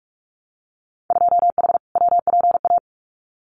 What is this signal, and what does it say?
Morse code, heard as a keyed tone, about 46 wpm, 700 Hz. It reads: 2H WPA
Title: Winter Field Day Exchanges 45wpm